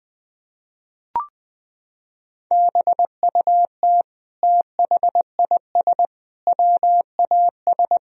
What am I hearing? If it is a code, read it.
Morse code, heard as a keyed tone, about 20 wpm, 700 Hz. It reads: BUT THIS WAS
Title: Sentences from Top 1000 Words - Review Part 4 - 20wpm